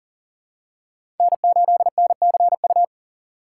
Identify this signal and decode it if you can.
Morse code, heard as a keyed tone, about 40 wpm, 700 Hz. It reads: N8NCU
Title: USA Call Signs - 3x Character Spacing - 40wpm